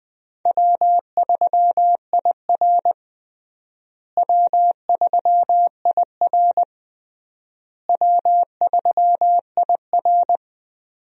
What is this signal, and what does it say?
Morse code, heard as a keyed tone, about 20 wpm, 700 Hz. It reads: W3IR W3IR W3IR
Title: USA Call Signs - 4 Characters - Repeated 3x - 20wpm